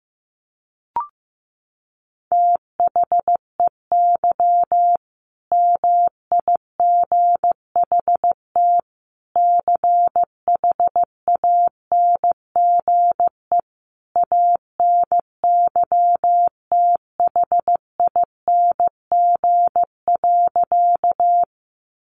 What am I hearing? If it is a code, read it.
Morse code, heard as a keyed tone, about 15 wpm, 700 Hz. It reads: THEY MIGHT CHANGE ANYTHING.
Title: A Sound Of Thunder 15wpm